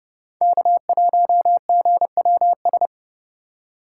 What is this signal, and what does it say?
Morse code, heard as a keyed tone, about 30 wpm, 700 Hz. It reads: K1GWS